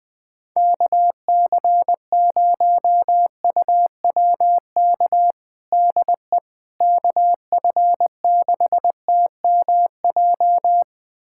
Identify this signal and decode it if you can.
Morse code, heard as a keyed tone, about 21 wpm, 700 Hz. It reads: KC0UWK DE KF6TMJ
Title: USA Callsign DE Callsign - 6 Characters - 20wpm